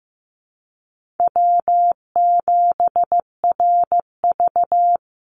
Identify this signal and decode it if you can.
Morse code, heard as a keyed tone, about 15 wpm, 700 Hz. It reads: W7RV
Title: NA SOTA Chaser Callsigns 15wpm